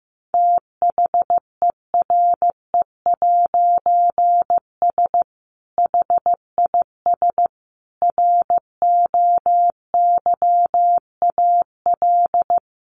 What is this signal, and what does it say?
Morse code, heard as a keyed tone, about 15 wpm, 700 Hz. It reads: THERE'S HIS ROYAL
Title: A Sound Of Thunder 15wpm